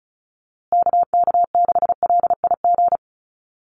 Morse code, heard as a keyed tone, about 34 wpm, 700 Hz. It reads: KK6LIG
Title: USA Call Signs Repeated 3x 35wpm